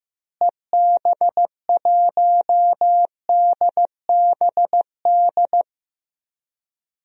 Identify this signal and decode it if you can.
Morse code, heard as a keyed tone, about 15 wpm, 700 Hz. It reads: EB1DBD
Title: DXCC-100 Call Signs - 6 Characters - Repeated 3x - 15wpm